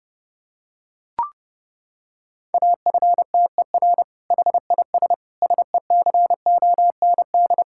Morse code, heard as a keyed tone, about 28 wpm, 700 Hz. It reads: AFTER HIS SECOND
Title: Sentences from Top 1000 Words - Review Part 6 - 30wpm